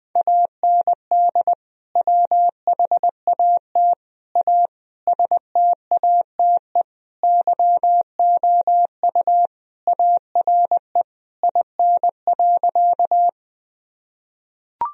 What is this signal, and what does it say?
Morse code, heard as a keyed tone, about 19 wpm, 700 Hz. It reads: AND WHAT A STATE YOU ARE IN.
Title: Sentences from Top 1000 Words - Review Part 2 - 20wpm